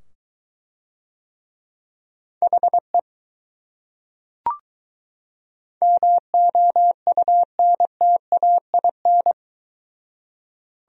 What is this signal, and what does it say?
Morse code, heard as a keyed tone, about 22 wpm, 700 Hz. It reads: HE MOUNTAIN